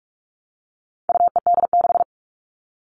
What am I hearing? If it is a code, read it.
Morse code, heard as a keyed tone, about 44 wpm, 700 Hz. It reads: UED6
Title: All Cut Numbers - 4-Digit 45wpm